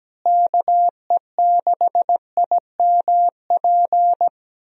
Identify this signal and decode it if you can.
Morse code, heard as a keyed tone, about 17 wpm, 700 Hz. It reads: KE6IMP
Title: USA Call Signs - 6 Characters - Repeated 3x - 17wpm